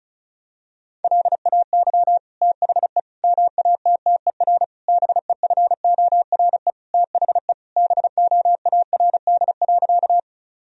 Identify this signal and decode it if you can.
Morse code, heard as a keyed tone, about 34 wpm, 700 Hz. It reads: LAY THE MATTER BEFORE THE BOARD.
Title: Sentences from Top 1000 Words - Review Part 4 - 35wpm